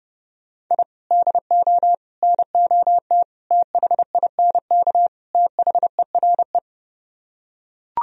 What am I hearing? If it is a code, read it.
Morse code, heard as a keyed tone, about 30 wpm, 700 Hz. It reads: I DO NOT THINK THERE